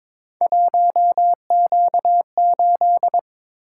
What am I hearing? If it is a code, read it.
Morse code, heard as a keyed tone, about 23 wpm, 700 Hz. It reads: 1Q8
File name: practice / 22wpm / Three Letter-Number Combinations - 22wpm